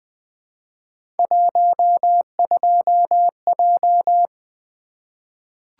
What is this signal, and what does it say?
Morse code, heard as a keyed tone, about 20 wpm, 700 Hz. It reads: 12J
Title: Three Letter-Number Combinations - 20wpm